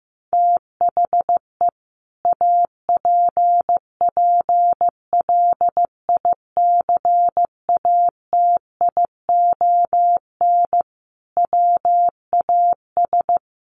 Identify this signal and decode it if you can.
Morse code, heard as a keyed tone, about 14 wpm, 700 Hz. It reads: THE APPLICATION WAS
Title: Sentences from Top 1200 Words 15wpm